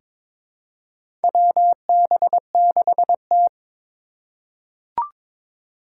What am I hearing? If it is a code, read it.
Morse code, heard as a keyed tone, about 22 wpm, 700 Hz. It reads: WB6T
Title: USA Call Signs - 3x Character Spacing - 22wpm